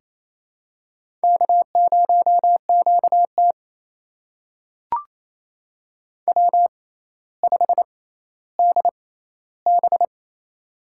Morse code, heard as a keyed tone, about 28 wpm, spaced as for 14 wpm, 700 Hz. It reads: K0QT W5DB
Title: USA Call Signs - 4 Characters - 6x Character Spacing - 28wpm